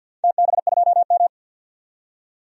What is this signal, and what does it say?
Morse code, heard as a keyed tone, about 47 wpm, 700 Hz. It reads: TB2M
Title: DXCC-100 Call Signs - 4 Characters - 50wpm